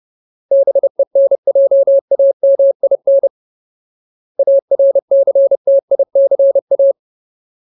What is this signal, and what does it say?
Morse code, heard as a keyed tone, about 30 wpm, 550 Hz. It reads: BENJAMIN ARCTICA
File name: Ultimate Word List / 30WPM Ultimate